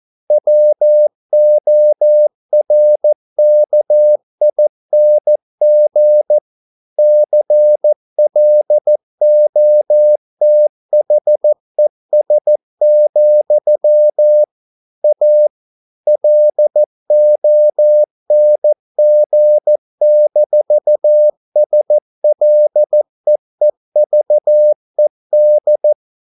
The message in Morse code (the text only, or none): WORKING CLOTHES, A LONG-SLEEVED